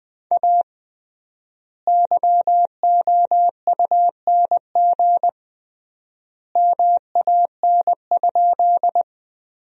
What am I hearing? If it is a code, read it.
Morse code, heard as a keyed tone, about 20 wpm, 700 Hz. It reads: A YOUNG MAN?
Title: 3x Word-Spacing - Sentences from Top 400 Words - 20wpm